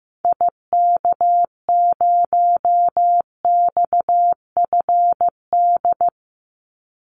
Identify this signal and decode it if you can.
Morse code, heard as a keyed tone, about 15 wpm, 700 Hz. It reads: IK0XFD
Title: Call Signs Repeated 3x - 15wpm